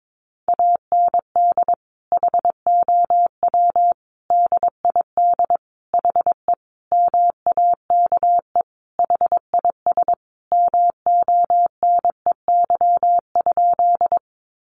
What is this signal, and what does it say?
Morse code, heard as a keyed tone, about 22 wpm, 700 Hz. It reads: AND HOW DID HE MAKE HIS MONEY?
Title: Sentences from Top 400 Words 22wpm